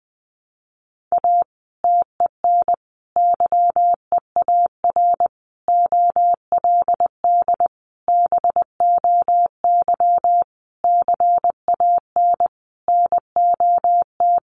Morse code, heard as a keyed tone, about 20 wpm, 700 Hz. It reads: A TEN YEAR OLD BOY CAN NOT
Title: Sentences from Top 1300 Words 20wpm